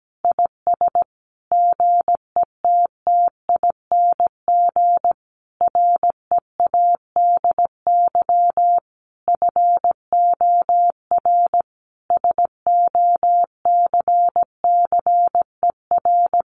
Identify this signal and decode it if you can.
Morse code, heard as a keyed tone, about 17 wpm, 700 Hz. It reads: IS GETTING READY FOR SOCCER
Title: Easy Phrases 17wpm